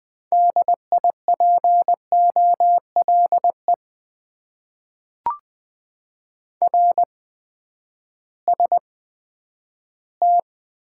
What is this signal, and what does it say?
Morse code, heard as a keyed tone, about 20 wpm, spaced as for 11 wpm, 700 Hz. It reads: DIPOLE RST
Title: Top 90 QSO Elements - 8x Character Spacing - 20wpm